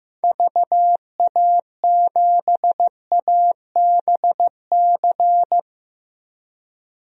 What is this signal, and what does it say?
Morse code, heard as a keyed tone, about 15 wpm, 700 Hz. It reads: VA7ABC
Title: DXCC-100 Call Signs - 6 Characters - Repeated 2x - 15wpm